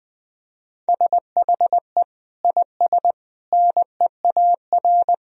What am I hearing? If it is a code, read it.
Morse code, heard as a keyed tone, about 20 wpm, 700 Hz. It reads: SHE IS NEAR
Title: Sentences from Top 1000 Words - Review Part 1 - 20wpm